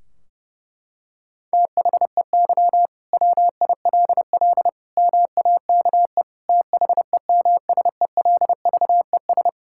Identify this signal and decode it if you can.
Morse code, heard as a keyed tone, about 30 wpm, 700 Hz. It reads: THEY WILL MAKE THEMSELVES
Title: Sentences from Top 1600 Words 30wpm